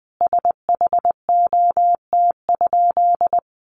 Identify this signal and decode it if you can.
Morse code, heard as a keyed tone, about 19 wpm, 700 Hz. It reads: SHOT?
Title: Sentences from Top 1100 Words 20wpm